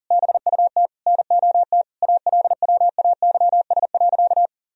Morse code, heard as a keyed tone, about 40 wpm, 700 Hz. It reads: BUT NOT ALWAYS.
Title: Sentences from Top 1000 Words - Review Part 4 - 40wpm